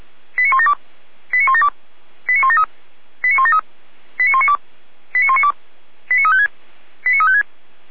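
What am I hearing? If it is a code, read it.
Five-tone selective call (ZVEI), heard as quick runs of five tones, tones of about 70 ms. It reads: 89172 89172 89173 89173 89192 89192 89356 89356
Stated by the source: ZVEI five-tone selective call, received on SDR.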